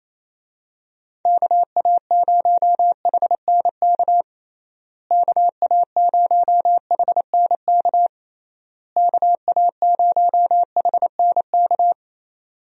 Morse code, heard as a keyed tone, about 27 wpm, 700 Hz. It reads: KA0HNK KA0HNK KA0HNK
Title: USA Call Signs - 6 Characters - Repeated 3x - 28wpm